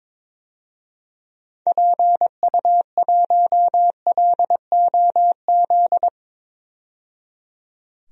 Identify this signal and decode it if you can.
Morse code, heard as a keyed tone, about 22 wpm, 700 Hz. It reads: PU1LOZ
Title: DXCC-100 Call Signs - 6 Characters - 22wpm